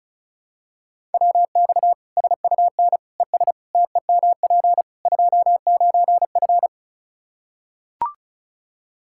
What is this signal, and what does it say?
Morse code, heard as a keyed tone, about 34 wpm, 700 Hz. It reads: WX SUN ES TEMP 29F
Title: Practice QSO - Exchange 2 - Info - 35wpm